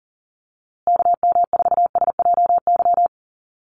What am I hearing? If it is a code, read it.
Morse code, heard as a keyed tone, about 40 wpm, 700 Hz. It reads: KM4SJY
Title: USA Call Signs Repeated 2x 40wpm